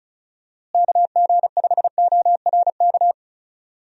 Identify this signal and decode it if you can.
Morse code, heard as a keyed tone, about 36 wpm, 700 Hz. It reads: KG5ORK
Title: USA Call Signs - 6 Characters - Repeated 2x - 35wpm